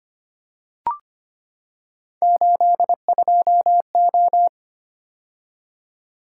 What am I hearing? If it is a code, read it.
Morse code, heard as a keyed tone, about 25 wpm, 700 Hz. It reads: 82O
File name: Three Letter-Number Combinations - 25wpm